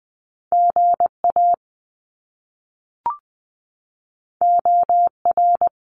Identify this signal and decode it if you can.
Morse code, heard as a keyed tone, about 20 wpm, 700 Hz. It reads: GA OR
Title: Top Two Letter Combinations in Words 20wpm